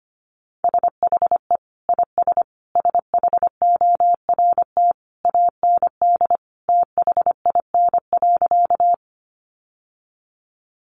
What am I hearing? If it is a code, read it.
Morse code, heard as a keyed tone, about 25 wpm, 700 Hz. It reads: SHE IS SHORT AND THIN.
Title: Sentences from Top 1200 Words 25wpm